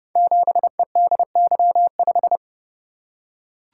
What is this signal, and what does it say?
Morse code, heard as a keyed tone, about 28 wpm, 700 Hz. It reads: 7EDY5
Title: Code Groups - 5 Letter:Numbers - 30wpm